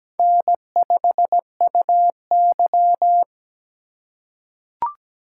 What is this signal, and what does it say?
Morse code, heard as a keyed tone, about 17 wpm, 700 Hz. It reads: N5UY
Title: USA Call Signs - 4 Characters - 3x Character Spacing - 17wpm